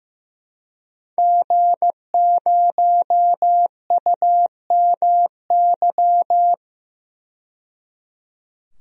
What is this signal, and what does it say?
Morse code, heard as a keyed tone, about 15 wpm, 700 Hz. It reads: G0UMY
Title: DXCC-100 Call Signs - 5 Characters - 15wpm